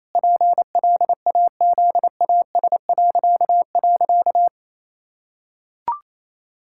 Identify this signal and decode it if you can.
Morse code, heard as a keyed tone, about 28 wpm, 700 Hz. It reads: PLAZAS..
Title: Pangram Challenge v2 - 28wpm